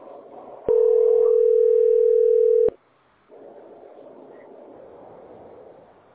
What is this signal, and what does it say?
Ringback tone (North America), 440 + 480 Hz.